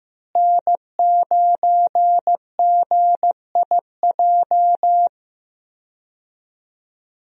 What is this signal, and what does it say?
Morse code, heard as a keyed tone, about 15 wpm, 700 Hz. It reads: N9GIJ
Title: USA Call Signs - 5 Characters - Repeated 2x - 15wpm